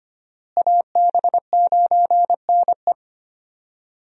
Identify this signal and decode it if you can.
Morse code, heard as a keyed tone, about 25 wpm, 700 Hz. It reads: AB9NE
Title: USA Call Signs - 5 Characters - Repeated 3x - 25wpm